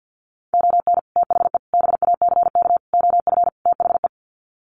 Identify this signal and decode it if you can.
Morse code, heard as a keyed tone, about 50 wpm, 700 Hz. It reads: ON THE BACK OF THE
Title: Sets of 5 Words 50wpm